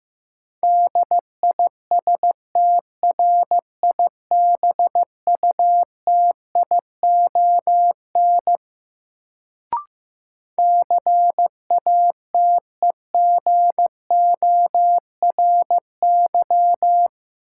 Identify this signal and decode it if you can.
Morse code, heard as a keyed tone, about 15 wpm, 700 Hz. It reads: DISTRIBUTION CATEGORY
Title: Sentences from Top 1000 Words - Review Part 7 - New Word Drill - 15wpm